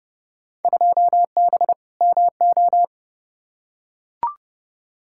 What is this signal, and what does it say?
Morse code, heard as a keyed tone, about 30 wpm, 700 Hz. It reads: 2B MO
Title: Field Day Exchanges Encore - 30wpm